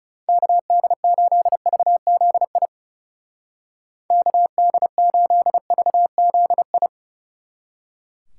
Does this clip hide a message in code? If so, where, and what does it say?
Yes: Morse code from 0.3 to 6.9 s, heard as a keyed tone, about 38 wpm, 700 Hz. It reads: KD8VZI KD8VZI